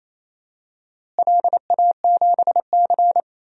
Morse code, heard as a keyed tone, about 28 wpm, 700 Hz. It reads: LA7C
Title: DXCC-100 Call Signs - 4 Characters - Repeated 2x - 28wpm